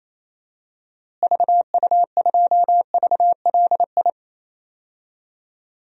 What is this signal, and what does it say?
Morse code, heard as a keyed tone, about 28 wpm, 700 Hz. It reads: VU2VLI
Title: DXCC-100 Call Signs - 6 Characters - Repeated 2x - 28wpm